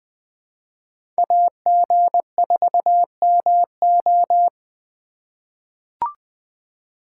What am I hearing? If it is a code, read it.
Morse code, heard as a keyed tone, about 20 wpm, 700 Hz. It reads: AG4MO
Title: USA Call Signs - 5 Characters - 3x Character Spacing - 20wpm